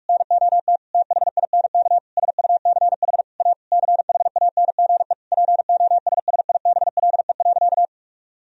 Morse code, heard as a keyed tone, about 41 wpm, 700 Hz. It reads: NOT THINK SUCH A CHANGE POSSIBLE.
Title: Sentences from Top 1000 Words - Review Part 1 - 45wpm